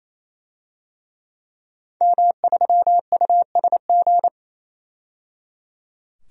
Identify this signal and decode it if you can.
Morse code, heard as a keyed tone, about 28 wpm, 700 Hz. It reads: M3USG
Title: DXCC-100 Call Signs - 5 Characters - 28wpm